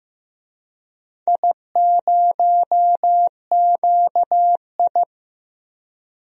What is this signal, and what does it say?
Morse code, heard as a keyed tone, about 15 wpm, 700 Hz. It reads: I0QI
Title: DXCC-100 Call Signs - 4 Characters - 15wpm